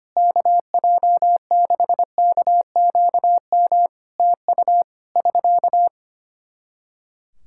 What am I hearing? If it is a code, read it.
Morse code, heard as a keyed tone, about 25 wpm, 700 Hz. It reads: KJ6KQM TU <SK>
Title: Practice QSO - Exchange 4 - Ending - 25wpm